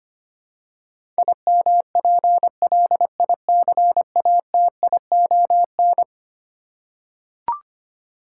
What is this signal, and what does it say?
Morse code, heard as a keyed tone, about 25 wpm, 700 Hz. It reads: IMPLICATION
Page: morse-code-ninja-practice-25wpm